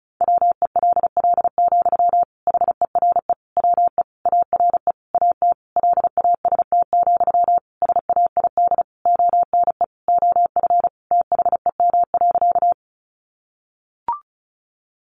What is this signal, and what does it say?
Morse code, heard as a keyed tone, about 34 wpm, 700 Hz. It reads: WELL, HERE WE ARE AT LAST, SAID ONE OF THEM.
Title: Sentences from Top 300 Words 35wpm